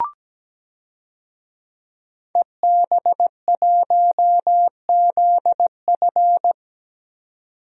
Morse code, heard as a keyed tone, about 17 wpm, 700 Hz. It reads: EB1ZF
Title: DXCC-100 Call Signs - 5 Characters - 17wpm